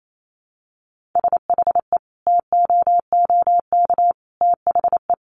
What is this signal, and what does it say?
Morse code, heard as a keyed tone, about 28 wpm, 700 Hz. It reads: SHE TOOK THE
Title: Sentences from Top 1400 Words 28wpm